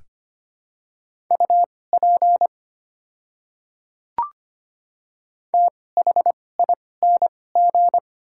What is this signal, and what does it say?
Morse code, heard as a keyed tone, about 25 wpm, spaced as for 18 wpm, 690 Hz. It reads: UP THING
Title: 2x Character Spacing - Top 100 Words - 25wpm